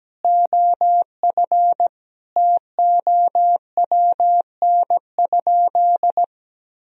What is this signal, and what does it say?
Morse code, heard as a keyed tone, about 17 wpm, 700 Hz. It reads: OF TOWN?
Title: Sentences from Top 1000 Words - Review Part 3 - 17wpm